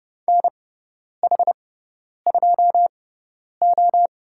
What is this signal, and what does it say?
Morse code, heard as a keyed tone, about 30 wpm, spaced as for 10 wpm, 700 Hz. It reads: NH2O
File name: Lesson 16g - Letter c - Call Signs 30wpm